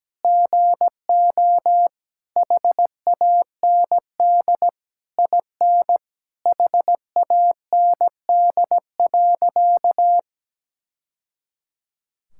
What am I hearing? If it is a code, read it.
Morse code, heard as a keyed tone, about 17 wpm, 700 Hz. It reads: GO HAND IN HAND.
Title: Sentences from Top 1000 Words - Review Part 5 - 17wpm